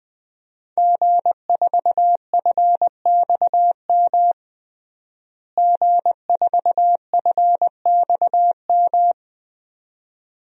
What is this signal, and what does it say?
Morse code, heard as a keyed tone, about 21 wpm, 700 Hz. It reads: G4FXM G4FXM
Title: DXCC-100 Call Signs - 5 Characters - Repeated 2x - 20wpm